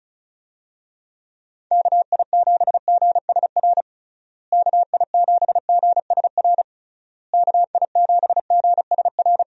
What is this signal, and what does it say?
Morse code, heard as a keyed tone, about 36 wpm, 700 Hz. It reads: KI7GSR KI7GSR KI7GSR
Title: USA Call Signs - 6 Characters - Repeated 3x - 35wpm